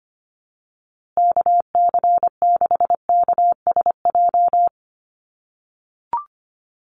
Morse code, heard as a keyed tone, about 25 wpm, 700 Hz. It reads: KC6KSJ
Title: USA Call Signs Repeated 3x 25wpm